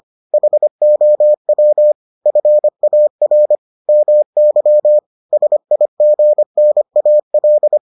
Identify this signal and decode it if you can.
Morse code, heard as a keyed tone, about 25 wpm, 600 Hz. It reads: HOW FAR MY SIGNAL